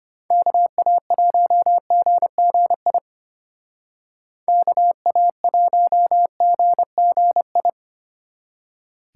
Morse code, heard as a keyed tone, about 30 wpm, 700 Hz. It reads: KA1GGI KA1GGI
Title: Speed Racing - USA 2x3 Call Signs 25wpm